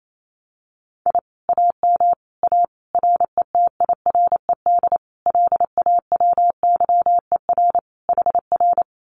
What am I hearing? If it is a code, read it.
Morse code, heard as a keyed tone, about 28 wpm, 700 Hz. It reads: I AM A RETIRED LAWYER HR
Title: Common QSO Phrases Encore 28wpm